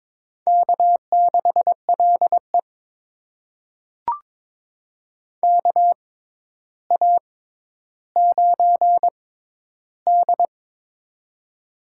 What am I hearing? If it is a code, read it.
Morse code, heard as a keyed tone, about 22 wpm, spaced as for 11 wpm, 700 Hz. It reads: K6LE KA9D
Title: USA Call Signs - 4 Characters - 6x Character Spacing - 22wpm